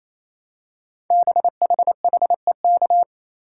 Morse code, heard as a keyed tone, about 29 wpm, 700 Hz. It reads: BHHEK
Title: Code Groups - 5 Letters - 28wpm